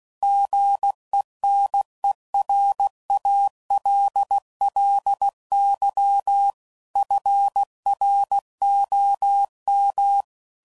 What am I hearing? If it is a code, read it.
Morse code, heard as a keyed tone, about 16 wpm, 800 Hz. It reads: GENERALLY FROM
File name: CW Code Groups @15WPM / Weather Sets / Weather 4 10